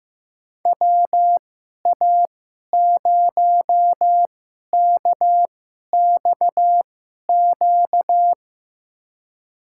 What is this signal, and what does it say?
Morse code, heard as a keyed tone, about 15 wpm, spaced as for 11 wpm, 700 Hz. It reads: WA0KXQ
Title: USA Call Signs - 6 Characters - 2x Character Spacing - 15wpm